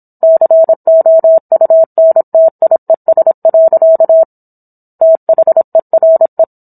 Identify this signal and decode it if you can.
Morse code, heard as a keyed tone, about 27 wpm, 650 Hz. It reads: COUNTIES. THERE